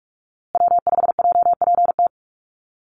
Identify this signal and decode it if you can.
Morse code, heard as a keyed tone, about 44 wpm, 700 Hz. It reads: W5JPT
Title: USA Call Signs Repeated 2x 45wpm